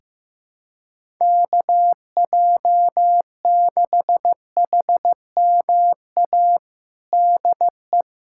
Morse code, heard as a keyed tone, about 15 wpm, 700 Hz. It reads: KJ6HMA DE
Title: USA Callsign DE Callsign - 6 Characters - 15wpm